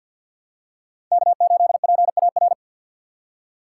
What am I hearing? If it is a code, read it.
Morse code, heard as a keyed tone, about 52 wpm, 700 Hz. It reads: K8PAR
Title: USA Call Signs - 4x Character Spacing - 50wpm